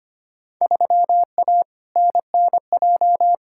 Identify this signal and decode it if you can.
Morse code, heard as a keyed tone, about 25 wpm, 700 Hz. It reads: 3A NNJ
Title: Field Day Exchanges Encore - 25wpm